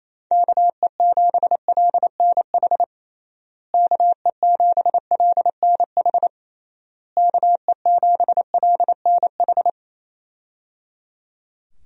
Morse code, heard as a keyed tone, about 28 wpm, 700 Hz. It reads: KE7LNH KE7LNH KE7LNH
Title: USA Call Signs - 6 Characters - Repeated 3x - 28wpm